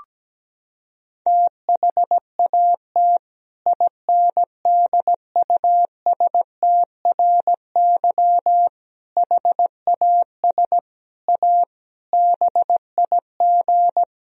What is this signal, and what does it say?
Morse code, heard as a keyed tone, about 17 wpm, 700 Hz. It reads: THAT INDUSTRY HAS A BIG
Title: Sentences from Top 1000 Words - Review Part 4 - 17wpm